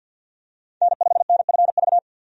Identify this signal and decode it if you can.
Morse code, heard as a keyed tone, about 48 wpm, 700 Hz. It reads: N5NFV
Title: USA Call Signs - 5 Characters - 50wpm